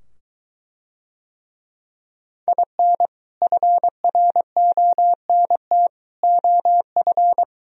Morse code, heard as a keyed tone, about 24 wpm, 700 Hz. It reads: IN FRONT OF